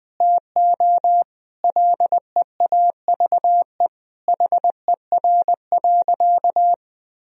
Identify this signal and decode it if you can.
Morse code, heard as a keyed tone, about 20 wpm, 700 Hz. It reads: TO LEAVE HER.
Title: Sentences from Top 1000 Words - Review Part 1 - 20wpm